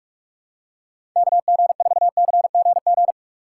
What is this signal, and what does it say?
Morse code, heard as a keyed tone, about 42 wpm, 700 Hz. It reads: KG4CGG
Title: USA Call Signs - 6 Characters - 2x Character Spacing - 45wpm